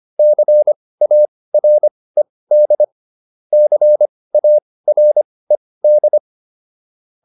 Morse code, heard as a keyed tone, about 25 wpm, spaced as for 18 wpm, 600 Hz. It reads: CARED CARED